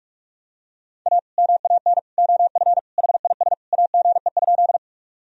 Morse code, heard as a keyed tone, about 46 wpm, 700 Hz. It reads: A MAN OF HIS AGE?
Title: Sentences from Top 1000 Words - Review Part 3 - 45wpm